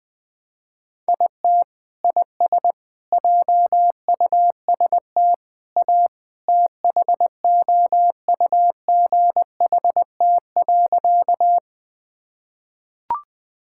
Morse code, heard as a keyed tone, about 20 wpm, 700 Hz. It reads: IT IS JUST A THOUGHT.
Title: Sentences with 2-Word Phrases - Spoken Beginning - 20wpm